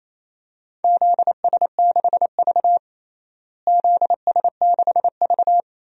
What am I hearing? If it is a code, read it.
Morse code, heard as a keyed tone, about 28 wpm, 700 Hz. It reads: ZS6V ZS6V
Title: DXCC-100 Call Signs - 4 Characters - Repeated 2x - 28wpm